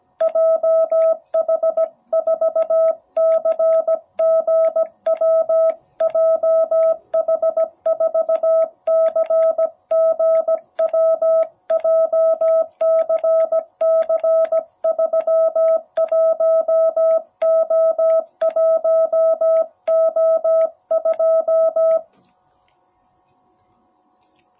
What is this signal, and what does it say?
Morse code, heard as a keyed tone, about 17 wpm, 660 Hz. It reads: JH4CGW JH4CGW JCC31O1O2
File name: jh8cbh / 2021cw